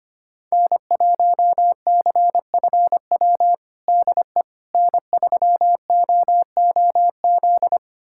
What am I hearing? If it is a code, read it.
Morse code, heard as a keyed tone, about 25 wpm, 700 Hz. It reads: N1CFW DE N3OOZ
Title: USA Callsign DE Callsign 25wpm